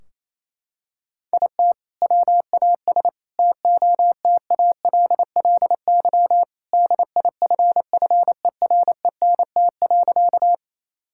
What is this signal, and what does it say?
Morse code, heard as a keyed tone, about 27 wpm, 700 Hz. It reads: IT WAS TOTALLY DIFFERENT.
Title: Sentences from Top 1600 Words 28wpm